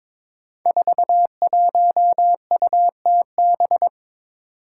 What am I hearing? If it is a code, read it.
Morse code, heard as a keyed tone, about 22 wpm, 700 Hz. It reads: 41UTB
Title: Five Challenging Letter-Number Combinations 22wpm